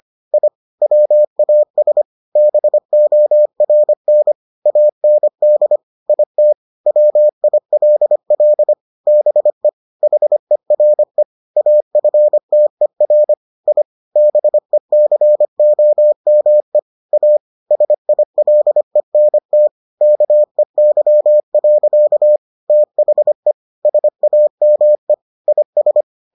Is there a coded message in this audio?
Yes: Morse code from 0.3 to 26.0 s, heard as a keyed tone, about 25 wpm, 600 Hz. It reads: I WAS BORN AND IT WILL BE HERE AFTER I BECOME A SILENT KEY. THE SAME IS